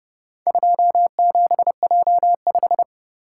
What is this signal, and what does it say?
Morse code, heard as a keyed tone, about 26 wpm, 700 Hz. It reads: 27J5